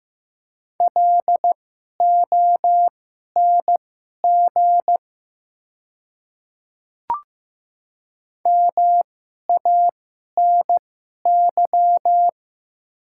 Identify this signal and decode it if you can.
Morse code, heard as a keyed tone, about 15 wpm, spaced as for 11 wpm, 700 Hz. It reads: LONG MANY
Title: 2x Character Spacing - Top 300 Words - 15wpm